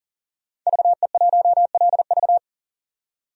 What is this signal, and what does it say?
Morse code, heard as a keyed tone, about 40 wpm, 700 Hz. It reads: VE1LV
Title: DXCC-100 Call Signs - 5 Characters - 40wpm